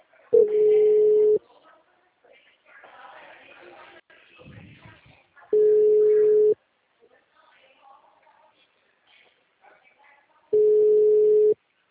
Ringback tone (Europe, 425 Hz).
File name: out- -2036- - - .62032